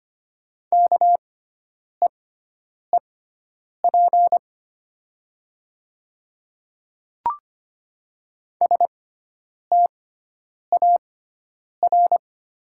Morse code, heard as a keyed tone, about 25 wpm, spaced as for 9 wpm, 700 Hz. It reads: KEEP STAR
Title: 6x Character Spacing - Top 500 Words - 25wpm